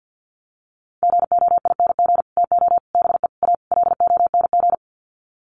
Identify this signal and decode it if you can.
Morse code, heard as a keyed tone, about 50 wpm, 700 Hz. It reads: GOING TO BE A LONG
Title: Sets of 5 Words Encore2 50wpm